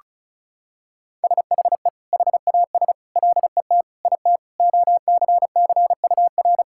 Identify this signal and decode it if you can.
Morse code, heard as a keyed tone, about 34 wpm, 700 Hz. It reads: SHE HAS LET IT OCCUR
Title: Sentences from Top 1000 Words - Review Part 6 - 35wpm